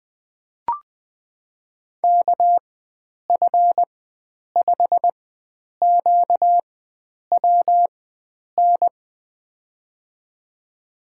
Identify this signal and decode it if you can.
Morse code, heard as a keyed tone, about 20 wpm, spaced as for 9 wpm, 700 Hz. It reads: KF5QWN
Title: USA Call Signs - 6 Characters - 4x Character Spacing - 20wpm